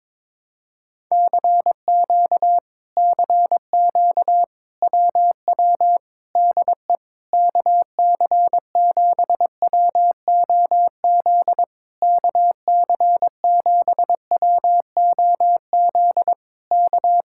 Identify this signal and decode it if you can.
Morse code, heard as a keyed tone, about 21 wpm, 700 Hz. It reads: CQ CQ WW DE KC7WOZ KC7WOZ K
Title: Speed-Racing Common QSO Phrases - 22wpm